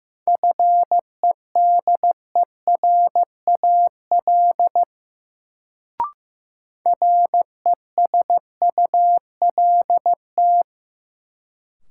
Morse code, heard as a keyed tone, about 15 wpm, 700 Hz. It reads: FEDERAL RESULT
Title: Sets of 3 Words - New Word Drill - 15wpm